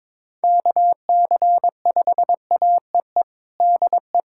Morse code, heard as a keyed tone, about 22 wpm, 700 Hz. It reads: KC5AEE DE
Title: USA Callsign DE Callsign - 6 Characters - 22wpm